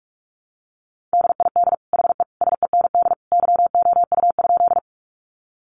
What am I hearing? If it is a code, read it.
Morse code, heard as a keyed tone, about 44 wpm, 700 Hz. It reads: DID HE SEND YOU?
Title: Sentences from Top 1500 Words 45wpm